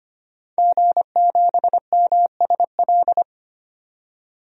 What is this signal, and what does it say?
Morse code, heard as a keyed tone, about 25 wpm, 700 Hz. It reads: G7MSL
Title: DXCC-100 Call Signs - 5 Characters - 25wpm